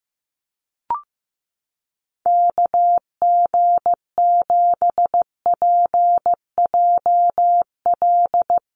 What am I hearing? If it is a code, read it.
Morse code, heard as a keyed tone, about 15 wpm, 700 Hz. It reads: KG7PJL
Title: USA Call Signs Repeated 3x 15wpm